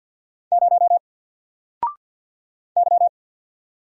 Morse code, heard as a keyed tone, about 50 wpm, 700 Hz. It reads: 0 Y
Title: Single Letter-Number Alt 50wpm